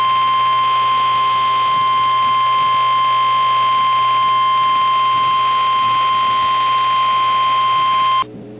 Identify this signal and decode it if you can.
Emergency Alert System (SAME) tones (the 1050 Hz attention tone), with noise. It is heard from 0.4 to 8.1 s.
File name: eas / KNCYKBIE / KNCYKBIE